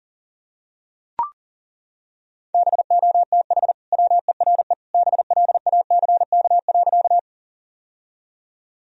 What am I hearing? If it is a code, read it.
Morse code, heard as a keyed tone, about 40 wpm, 700 Hz. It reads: BOTH WERE BLACK.
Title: Sentences from Top 1000 Words - Review Part 2 - 40wpm